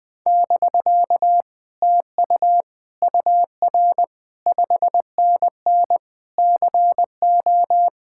Morse code, heard as a keyed tone, about 20 wpm, 700 Hz. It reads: <BK> TU UR 5NN CO
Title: Speed-Racing Common QSO Phrases Encore - 17wpm